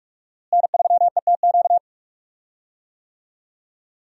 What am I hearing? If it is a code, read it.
Morse code, heard as a keyed tone, about 47 wpm, 700 Hz. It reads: N3ETQ
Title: USA Call Signs - 5 Characters - Repeated 2x - 45wpm